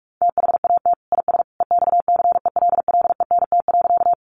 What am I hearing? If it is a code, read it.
Morse code, heard as a keyed tone, about 44 wpm, 700 Hz. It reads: THAT IS EXCELLENT.
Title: Sentences from Top 1000 Words 45wpm